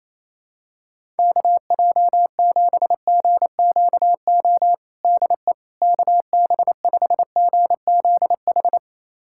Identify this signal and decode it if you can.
Morse code, heard as a keyed tone, about 28 wpm, 700 Hz. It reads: KJ7GQO DE KB5GZH
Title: USA Callsign DE Callsign - 6 Characters - 28wpm